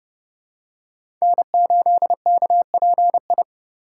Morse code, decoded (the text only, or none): N8KPI